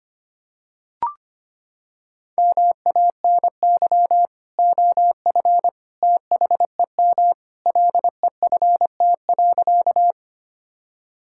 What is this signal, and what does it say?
Morse code, heard as a keyed tone, about 25 wpm, 700 Hz. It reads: MANY OF THEM LEFT.
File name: practice / 25wpm / Sentences with 2-Word Phrases - 25wpm